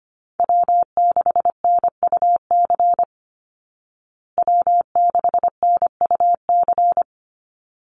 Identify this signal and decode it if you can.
Morse code, heard as a keyed tone, about 25 wpm, 700 Hz. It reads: W6NUC W6NUC
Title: Call Signs Repeated 3x - 25wpm